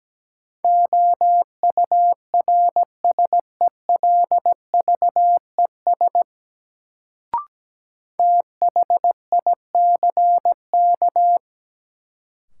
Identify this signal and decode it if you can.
Morse code, heard as a keyed tone, about 17 wpm, 700 Hz. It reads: OURSELVES THICK
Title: Sentences from Top 1600 Words - New Word Drill - 17wpm